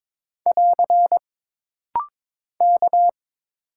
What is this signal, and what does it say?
Morse code, heard as a keyed tone, about 22 wpm, 700 Hz. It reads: <AR> K
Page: This rapid-fire timing is geared to push your ICR (Instant Character Recognition) skills.